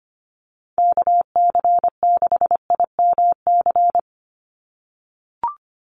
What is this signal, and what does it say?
Morse code, heard as a keyed tone, about 25 wpm, 690 Hz. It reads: KC6IMC
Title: USA Call Signs 25wpm